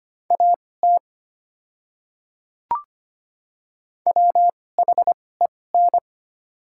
Morse code, heard as a keyed tone, about 25 wpm, spaced as for 18 wpm, 690 Hz. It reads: AT WHEN
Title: 2x Character Spacing - Top 100 Words - 25wpm